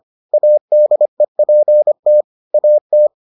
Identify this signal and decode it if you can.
Morse code, heard as a keyed tone, about 25 wpm, 600 Hz. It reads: ADEPT AT